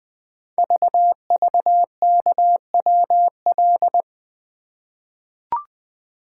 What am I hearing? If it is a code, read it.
Morse code, heard as a keyed tone, about 20 wpm, 700 Hz. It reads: VVKWL
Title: Code Groups - 5 Letters - 20wpm